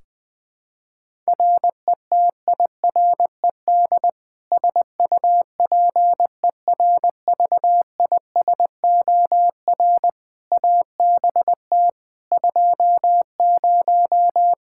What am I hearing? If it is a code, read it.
Morse code, heard as a keyed tone, about 20 wpm, 700 Hz. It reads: RETIRED SUPERVISOR ABT 20
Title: Practice QSO - Exchange 3 - More Info - 20wpm